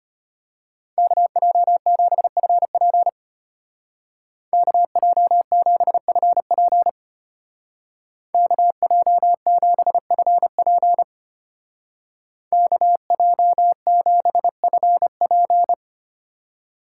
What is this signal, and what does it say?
Morse code, heard as a keyed tone, about 31 wpm, 700 Hz. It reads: KJ7FP KJ7FP KJ7FP KJ7FP